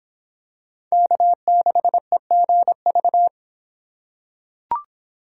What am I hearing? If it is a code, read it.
Morse code, heard as a keyed tone, about 25 wpm, 700 Hz. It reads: K6EGV